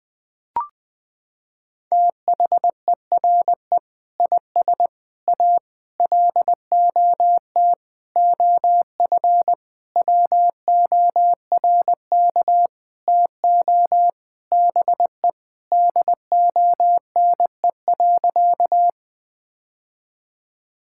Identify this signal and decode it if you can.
Morse code, heard as a keyed tone, about 20 wpm, 700 Hz. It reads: THERE IS A LOT OF WORK TO BE DONE.
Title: Sentences with 7-Word Phrases - 20wpm